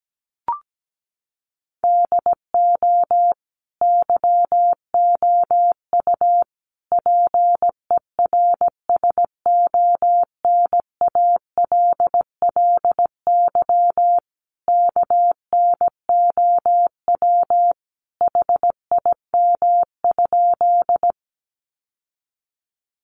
Morse code, heard as a keyed tone, about 17 wpm, 700 Hz. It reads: DO YOU PERSONALLY KNOW HIM?
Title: Sentences from Top 1400 Words 17wpm